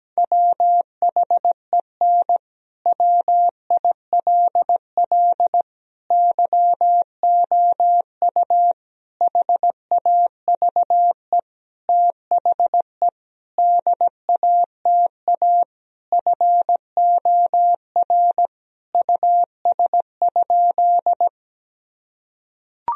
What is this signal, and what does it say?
Morse code, heard as a keyed tone, about 17 wpm, 700 Hz. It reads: WHEN WILL YOU HAVE THE DATA FOR US?
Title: Sentences from Top 1000 Words - Review Part 3 - 17wpm